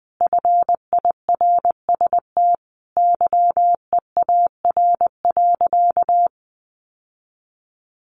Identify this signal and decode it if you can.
Morse code, heard as a keyed tone, about 20 wpm, 700 Hz. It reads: FIRST YEAR.
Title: Sentences from Top 1300 Words 20wpm